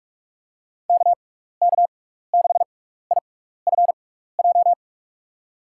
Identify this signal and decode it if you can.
Morse code, heard as a keyed tone, about 41 wpm, spaced as for 15 wpm, 700 Hz. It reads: KK6IFJ
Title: USA Call Signs - 6x Character Spacing - 45wpm